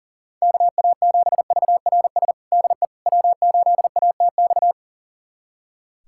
Morse code, heard as a keyed tone, about 37 wpm, 700 Hz. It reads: KA7VRS DE W8ATX
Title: USA Callsign DE Callsign 40wpm